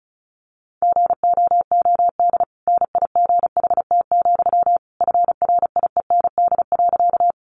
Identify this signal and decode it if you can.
Morse code, heard as a keyed tone, about 34 wpm, 700 Hz. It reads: GOOD NIGHT, FRIEND.
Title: Sentences from Top 300 Words 35wpm